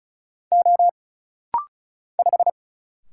Morse code, heard as a keyed tone, about 35 wpm, 700 Hz. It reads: O 5
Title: Single Letter-Number Alt - Rapid-Fire 35wpm